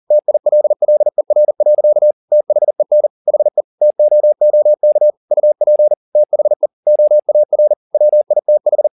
Morse code, heard as a keyed tone, about 40 wpm, 600 Hz. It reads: TILLER. THEN HE TOOK UP THE OAR WITH
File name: the old man and the sea 2 40wpm